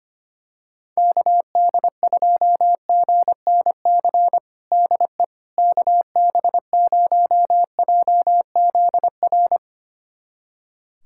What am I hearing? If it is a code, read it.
Morse code, heard as a keyed tone, about 25 wpm, 700 Hz. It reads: KD2GNC DE KB0JZR
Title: USA Callsign DE Callsign - 6 Characters - 25wpm